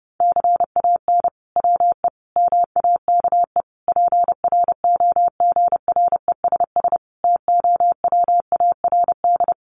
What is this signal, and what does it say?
Morse code, heard as a keyed tone, about 30 wpm, 700 Hz. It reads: CAN WE MAKE PROGRESS TOWARD
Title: Sentences from Top 1500 Words 30wpm